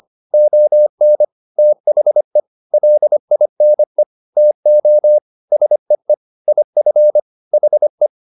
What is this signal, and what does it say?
Morse code, heard as a keyed tone, about 25 wpm, 600 Hz. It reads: ON THE LINE TO SEE IF HE
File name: the old man and the sea part 1 25wpm